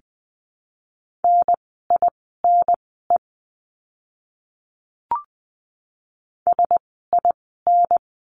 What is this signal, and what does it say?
Morse code, heard as a keyed tone, about 20 wpm, spaced as for 14 wpm, 700 Hz. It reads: NINE SIN
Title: 054 Lesson 10f - Letter r - Words 20wpm